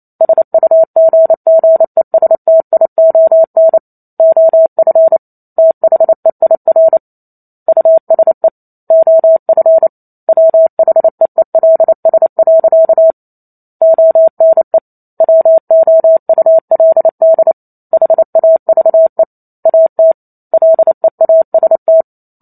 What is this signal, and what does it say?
Morse code, heard as a keyed tone, about 28 wpm, 650 Hz. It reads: SUGGESTION OF THEIR USE OF WHEELS. ONE WOULD HAVE AT LEAST
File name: Chapter 2 - 28 WPM